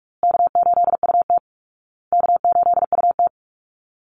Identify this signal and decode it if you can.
Morse code, heard as a keyed tone, about 44 wpm, 700 Hz. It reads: K8UT K8UT
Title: Easy Call Signs Repeated 3x - 45wpm